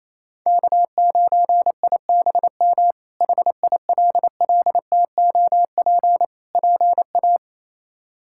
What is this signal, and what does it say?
Morse code, heard as a keyed tone, about 27 wpm, 700 Hz. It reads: K9IBM HILLTOP PA
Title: K1USN SST QSO Practice - 28wpm